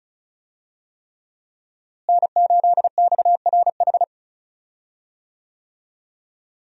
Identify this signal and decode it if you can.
Morse code, heard as a keyed tone, about 36 wpm, 700 Hz. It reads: N8XRH
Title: USA Call Signs - 5 Characters - Repeated 3x - 35wpm